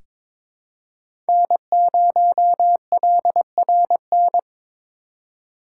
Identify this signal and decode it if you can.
Morse code, heard as a keyed tone, about 21 wpm, 700 Hz. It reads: N0LRN
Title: USA Call Signs - 5 Characters - 3x Character Spacing - 22wpm